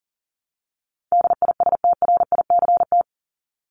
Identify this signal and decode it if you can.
Morse code, heard as a keyed tone, about 40 wpm, 700 Hz. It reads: DISTRICT
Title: Top 1501-1750 Words 40wpm